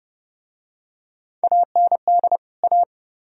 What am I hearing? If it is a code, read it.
Morse code, heard as a keyed tone, about 28 wpm, 700 Hz. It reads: AND A